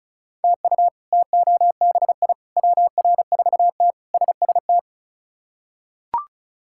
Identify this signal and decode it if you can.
Morse code, heard as a keyed tone, about 36 wpm, 700 Hz. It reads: TU TOBI WR4T SST
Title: K1USN SST QSO Practice - 35wpm